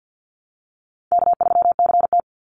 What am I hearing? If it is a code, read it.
Morse code, heard as a keyed tone, about 50 wpm, 700 Hz. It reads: K3CT
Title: Call Signs Repeated 3x - 50wpm